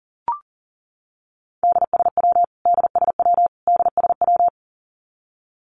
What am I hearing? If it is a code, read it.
Morse code, heard as a keyed tone, about 40 wpm, 700 Hz. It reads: DSW DSW DSW
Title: Abbreviations Repeated 3x - 40wpm